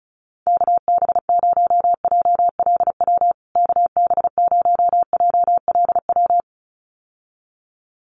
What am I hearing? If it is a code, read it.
Morse code, heard as a keyed tone, about 34 wpm, 700 Hz. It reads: KB0JLW KB0JLW
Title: USA Call Signs Repeated 2x 35wpm